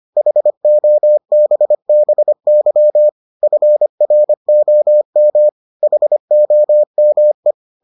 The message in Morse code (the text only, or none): HOBBY FROM HOME